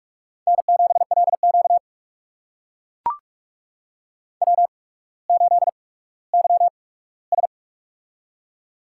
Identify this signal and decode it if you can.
Morse code, heard as a keyed tone, about 44 wpm, spaced as for 19 wpm, 700 Hz. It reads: N7LQ W8YS
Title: USA Call Signs - 4 Characters - 8x Character Spacing - 45wpm